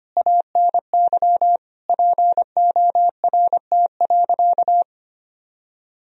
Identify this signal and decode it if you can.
Morse code, heard as a keyed tone, about 25 wpm, 700 Hz. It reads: ANY PORT.
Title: Sentences from Top 1000 Words - Review Part 2 - 25wpm